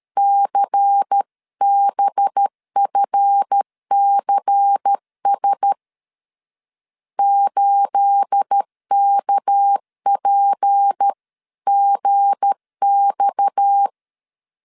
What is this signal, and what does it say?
Morse code, heard as a keyed tone, about 13 wpm, 800 Hz. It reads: CBFCS 8KPGX
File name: CW / DET / 1 9 1500C S A 12wpm 3800